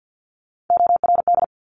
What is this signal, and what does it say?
Morse code, heard as a keyed tone, about 50 wpm, 700 Hz. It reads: ORD
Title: Top Three Letter Combinations in Words 50wpm